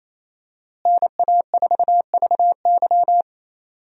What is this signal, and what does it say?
Morse code, heard as a keyed tone, about 28 wpm, 700 Hz. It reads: NA4VY
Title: USA Call Signs - 5 Characters - Repeated 2x - 28wpm